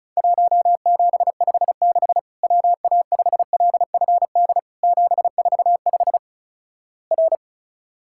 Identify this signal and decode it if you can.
Morse code, heard as a keyed tone, about 34 wpm, 700 Hz. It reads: 1756 WA5LFD 745 R
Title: Worked All Europe DX Contest - QTC Exchange - 35wpm